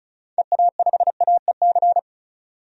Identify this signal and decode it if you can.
Morse code, heard as a keyed tone, about 34 wpm, 700 Hz. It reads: EA5AEC
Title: DXCC-100 Call Signs - 6 Characters - 35wpm